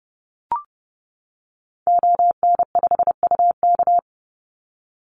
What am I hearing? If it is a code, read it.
Morse code, heard as a keyed tone, about 30 wpm, 700 Hz. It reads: ON5UK
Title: Call Signs - 30wpm